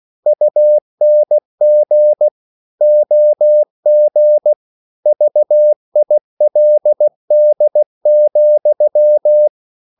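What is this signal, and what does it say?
Morse code, heard as a keyed tone, about 16 wpm, 600 Hz. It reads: UNG OG VILD,
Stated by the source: Slow (QRS) Morse code at 16 wpm, from CW med Gnister.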